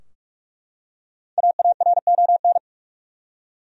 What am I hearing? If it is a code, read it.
Morse code, heard as a keyed tone, about 43 wpm, 700 Hz. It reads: AARON
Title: USA Common Names - Max 5 Letters - 45wpm